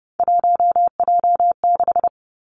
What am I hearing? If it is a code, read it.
Morse code, heard as a keyed tone, about 30 wpm, 700 Hz. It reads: 1J6
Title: Three Challenging Letter-Number Combinations 30wpm